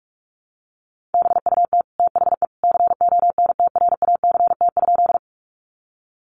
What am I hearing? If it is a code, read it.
Morse code, heard as a keyed tone, about 44 wpm, 700 Hz. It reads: BUT THE CONTRACT?
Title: Sentences from Top 1400 Words 45wpm